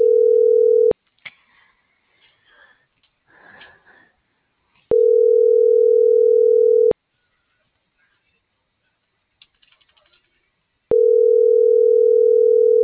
Ringback tone (North America), 440 + 480 Hz.